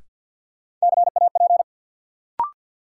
Morse code, heard as a keyed tone, about 47 wpm, 700 Hz. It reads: CAP
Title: Sentences from Top 1600 Words - New Word Drill Alt - 50wpm